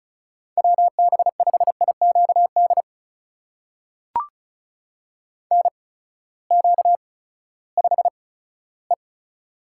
Morse code, heard as a keyed tone, about 36 wpm, spaced as for 18 wpm, 700 Hz. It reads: WB5IQD NQ5E
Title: USA Call Signs - 8x Character Spacing - 35wpm